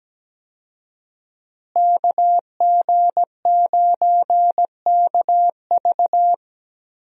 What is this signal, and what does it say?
Morse code, heard as a keyed tone, about 17 wpm, 700 Hz. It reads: KG9KV
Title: USA Call Signs - 5 Characters - Repeated 2x - 17wpm